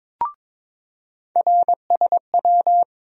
Morse code, heard as a keyed tone, about 22 wpm, 700 Hz. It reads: RSW
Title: USA Call Sign Postfixes - 3 Letters 22wpm